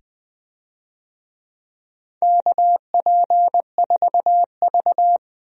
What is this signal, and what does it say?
Morse code, heard as a keyed tone, about 20 wpm, 700 Hz. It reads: KP4V
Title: DXCC-100 Call Signs - 4 Characters - 20wpm